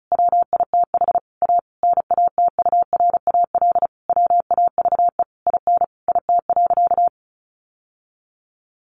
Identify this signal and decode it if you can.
Morse code, heard as a keyed tone, about 34 wpm, 700 Hz. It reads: WITH A NATURAL WAVE IN IT.
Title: Sentences from Top 1400 Words 35wpm